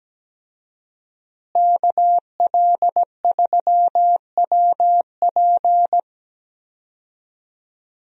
Morse code, heard as a keyed tone, about 17 wpm, 700 Hz. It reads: KL3WP
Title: USA Call Signs - 5 Characters - Repeated 2x - 17wpm